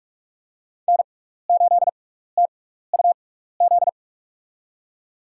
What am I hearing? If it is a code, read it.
Morse code, heard as a keyed tone, about 44 wpm, spaced as for 15 wpm, 700 Hz. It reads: N8TUZ
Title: USA Call Signs - 6x Character Spacing - 45wpm